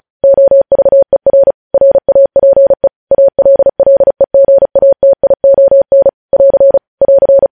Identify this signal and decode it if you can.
Morse code, heard as a keyed tone, about 36 wpm, 550 Hz. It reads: OVER RAPE ALLEGATION <AR> <AR>
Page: news headlines in morse code at 35 wpm